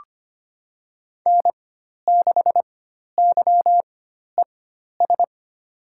Morse code, heard as a keyed tone, about 25 wpm, spaced as for 12 wpm, 700 Hz. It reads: N6YES
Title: USA Call Signs - 5 Characters - 4x Character Spacing - 25wpm